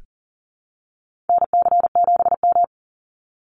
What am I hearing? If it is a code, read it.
Morse code, heard as a keyed tone, about 40 wpm, 700 Hz. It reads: NC7M
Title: Call Signs Repeated 3x - 40wpm